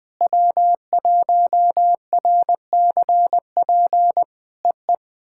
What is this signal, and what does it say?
Morse code, heard as a keyed tone, about 19 wpm, 700 Hz. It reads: W1RCP EE
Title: POTA QSO Practice - 20wpm